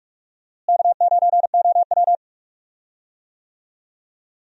Morse code, heard as a keyed tone, about 46 wpm, 700 Hz. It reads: K9OW
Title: USA Call Signs - 4 Characters - Repeated 3x - 45wpm